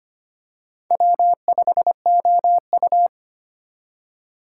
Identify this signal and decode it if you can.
Morse code, heard as a keyed tone, about 25 wpm, 700 Hz. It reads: W5OU
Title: USA Call Signs - 4 Characters - 4x Character Spacing - 25wpm